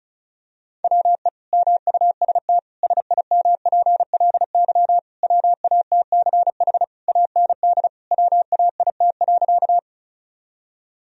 Morse code, heard as a keyed tone, about 36 wpm, 700 Hz. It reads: WE MUST SIMPLY WATCH AND WAIT.
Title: Sentences from Top 1000 Words - Review Part 4 - 35wpm